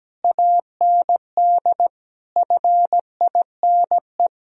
Morse code, heard as a keyed tone, about 17 wpm, 700 Hz. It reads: AND FINE
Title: Sentences from Top 1000 Words - Review Part 2 - 17wpm